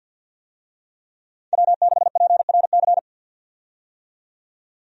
Morse code, heard as a keyed tone, about 53 wpm, 700 Hz. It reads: W6PRC
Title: USA Call Signs - 5 Characters - 50wpm